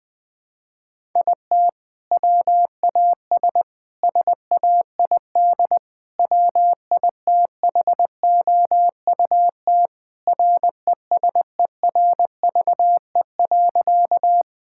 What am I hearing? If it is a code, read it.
Morse code, heard as a keyed tone, about 20 wpm, 700 Hz. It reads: IT WAS SAID WITHOUT RESERVE.
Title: Sentences from Top 1000 Words - Review Part 6 - 20wpm